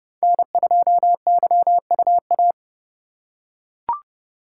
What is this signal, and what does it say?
Morse code, heard as a keyed tone, about 31 wpm, 700 Hz. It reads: N2YUA
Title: USA Call Signs - 5 Characters - 2x Character Spacing - 30wpm